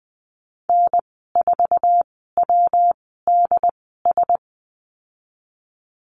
Morse code, heard as a keyed tone, about 20 wpm, spaced as for 15 wpm, 700 Hz. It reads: N4WDS
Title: 122 Lesson 20g - Number 3 - Call Signs 20wpm